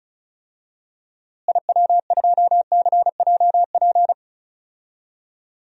Morse code, heard as a keyed tone, about 36 wpm, 700 Hz. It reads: IW2CJP
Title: DXCC-100 Call Signs - 6 Characters - Repeated 3x - 35wpm